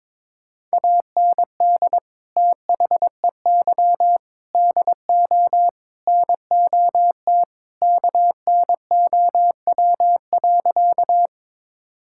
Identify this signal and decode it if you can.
Morse code, heard as a keyed tone, about 22 wpm, 700 Hz. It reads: AND THEY DO NOT KNOW.
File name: practice / 22wpm / Sentences with 4-Word Phrases - 22wpm